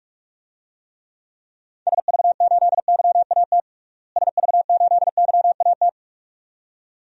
Morse code, heard as a keyed tone, about 42 wpm, 700 Hz. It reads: SV8YAT SV8YAT
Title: DXCC-100 Call Signs - 6 Characters - Repeated 2x - 45wpm